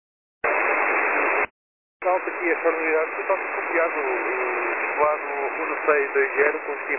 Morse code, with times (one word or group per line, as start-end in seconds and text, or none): none